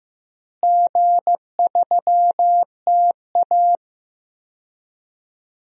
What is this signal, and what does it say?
Morse code, heard as a keyed tone, about 15 wpm, 700 Hz. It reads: G3TA
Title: DXCC-100 Call Signs - 4 Characters - Repeated 3x - 15wpm